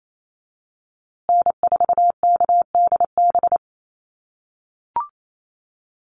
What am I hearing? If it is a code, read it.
Morse code, heard as a keyed tone, about 28 wpm, 700 Hz. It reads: N4KDB
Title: USA Call Signs Repeated 2x 28wpm